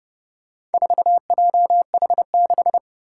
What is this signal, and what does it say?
Morse code, heard as a keyed tone, about 30 wpm, 700 Hz. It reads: 4JH6